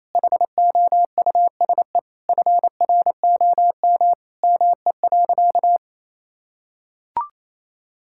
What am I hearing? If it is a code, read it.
Morse code, heard as a keyed tone, about 28 wpm, 700 Hz. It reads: HOUSE FROM ME.
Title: Sentences from Top 1000 Words - Review Part 5 - 28wpm